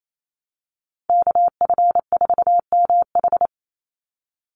Morse code, heard as a keyed tone, about 28 wpm, 700 Hz. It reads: KF4MH
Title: NA SOTA Chaser Callsigns 28wpm